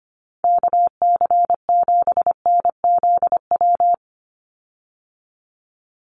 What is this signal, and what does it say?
Morse code, heard as a keyed tone, about 25 wpm, 700 Hz. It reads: KC7NZW
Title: USA Call Signs 25wpm